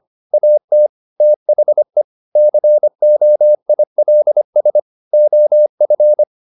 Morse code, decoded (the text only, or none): AT THE COILS OF